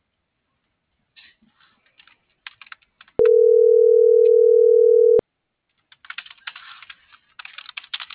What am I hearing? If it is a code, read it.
Ringback tone (North America), 440 + 480 Hz.